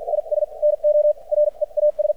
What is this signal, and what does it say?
Morse code, heard as a keyed tone, about 53 wpm, spaced as for 42 wpm, 600 Hz. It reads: IS NO REAS
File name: archived / CRSnet